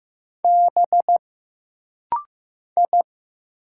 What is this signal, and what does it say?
Morse code, heard as a keyed tone, about 15 wpm, 700 Hz. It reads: B I
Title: Single Letters - Rapid-Fire 15wpm